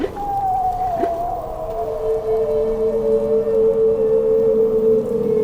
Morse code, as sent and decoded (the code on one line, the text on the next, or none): none